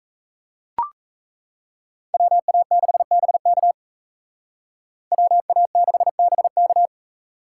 Morse code, heard as a keyed tone, about 33 wpm, 700 Hz. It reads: WA6BK WA6BK
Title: Speed Racing - USA 2x2 Call Signs 28wpm